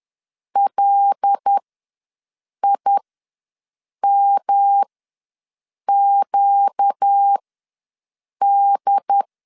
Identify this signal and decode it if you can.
Morse code, heard as a keyed tone, about 10 wpm, spaced as for 6 wpm, 800 Hz. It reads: LIMQD
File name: LEZIONI / 5 LBVC / 5 L ws 10wpm 3636